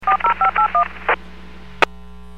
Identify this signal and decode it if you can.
DTMF, heard as telephone touch tones, tones of about 110 ms. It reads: 40581